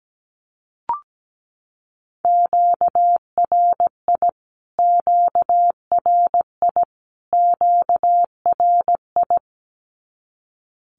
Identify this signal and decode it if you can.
Morse code, heard as a keyed tone, about 17 wpm, 700 Hz. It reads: QRI QRI QRI
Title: Q-Codes Repeated 3x 17wpm